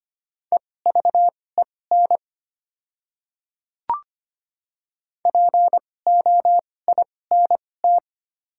Morse code, heard as a keyed tone, about 25 wpm, spaced as for 18 wpm, 700 Hz. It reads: EVEN POINT
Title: 2x Character Spacing - Top 300 Words - 25wpm